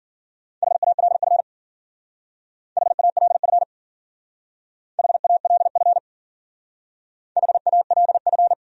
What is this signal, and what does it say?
Morse code, heard as a keyed tone, about 50 wpm, 700 Hz. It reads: HALF HALF HALF HALF
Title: Speed-Racing Top 500 Words 40wpm